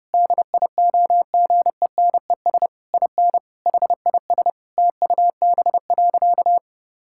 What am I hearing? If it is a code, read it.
Morse code, heard as a keyed tone, about 30 wpm, 700 Hz. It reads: DIOGENES IN HIS TUB.
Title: Pangram Challenge v2 - 30wpm